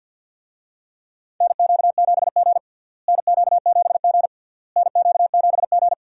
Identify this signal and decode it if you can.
Morse code, heard as a keyed tone, about 46 wpm, 700 Hz. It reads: NQ7G NQ7G NQ7G
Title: USA Call Signs - 4 Characters - Repeated 3x - 50wpm